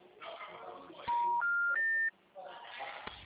Special information tone, with noise.